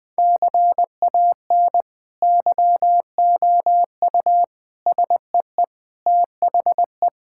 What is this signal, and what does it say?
Morse code, heard as a keyed tone, about 21 wpm, 700 Hz. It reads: CAN YOU SEE THE
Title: Sentences with 2-Word Phrases - Spoken Beginning - 20wpm